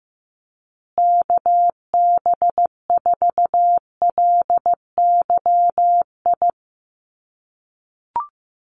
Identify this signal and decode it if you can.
Morse code, heard as a keyed tone, about 15 wpm, 700 Hz. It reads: KB4LYI
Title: USA Call Signs Repeated 2x 15wpm